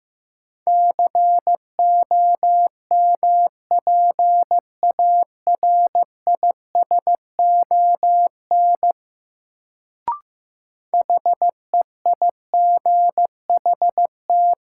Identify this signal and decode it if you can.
Morse code, heard as a keyed tone, about 15 wpm, 700 Hz. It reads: COMPARISON HEIGHT
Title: Sentences from Top 1600 Words - New Word Drill - 15wpm